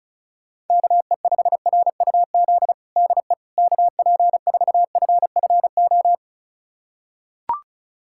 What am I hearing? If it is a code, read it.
Morse code, heard as a keyed tone, about 36 wpm, 700 Hz. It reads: KE5RUZ DE KP4FFO
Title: USA Callsign DE Callsign - 6 Characters - 35wpm